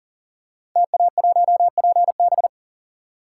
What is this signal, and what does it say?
Morse code, heard as a keyed tone, about 38 wpm, 700 Hz. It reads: TA1PB
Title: Code Groups - 5 Letter:Numbers - 40wpm